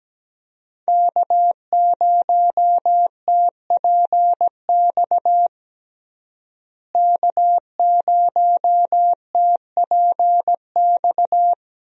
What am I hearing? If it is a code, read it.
Morse code, heard as a keyed tone, about 17 wpm, 700 Hz. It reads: K0TPX K0TPX
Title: USA Call Signs - 5 Characters - Repeated 2x - 17wpm